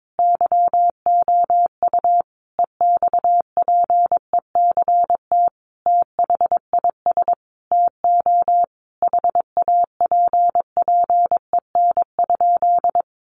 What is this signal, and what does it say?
Morse code, heard as a keyed tone, about 22 wpm, 700 Hz. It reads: YOU EXPECT THIS TO HAPPEN?
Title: Sentences from Top 1100 Words 22wpm